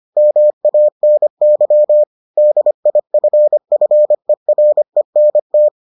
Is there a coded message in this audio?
Yes: Morse code from 0.2 to 5.7 s, heard as a keyed tone, about 25 wpm, 600 Hz. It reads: MANY DIFFERENT